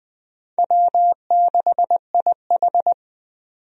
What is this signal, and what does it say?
Morse code, heard as a keyed tone, about 20 wpm, 700 Hz. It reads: W6IH
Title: USA Call Signs - 4 Characters - 20wpm